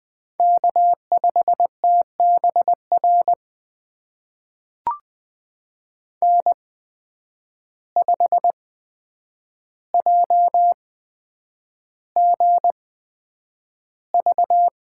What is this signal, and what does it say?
Morse code, heard as a keyed tone, about 20 wpm, spaced as for 9 wpm, 700 Hz. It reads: K5TBR N5JGV
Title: USA Call Signs - 8x Character Spacing - 20wpm